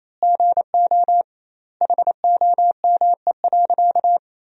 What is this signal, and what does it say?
Morse code, heard as a keyed tone, about 28 wpm, 700 Hz. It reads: GO HOME.
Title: 2x Word-Spacing - Sentences from Top 300 Words - 28wpm